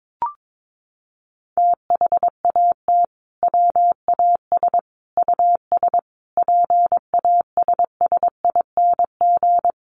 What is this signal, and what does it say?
Morse code, heard as a keyed tone, about 22 wpm, 700 Hz. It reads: THAT WAS US PASSING
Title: A Sound Of Thunder 22wpm